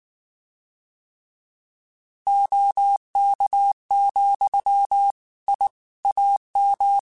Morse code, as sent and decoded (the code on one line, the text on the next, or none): --- -.- --..-- / .. / .- --
OK, I AM